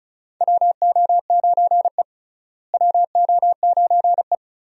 Morse code, heard as a keyed tone, about 34 wpm, 700 Hz. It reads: WO9E WO9E
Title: USA Call Signs - 4 Characters - Repeated 2x - 35wpm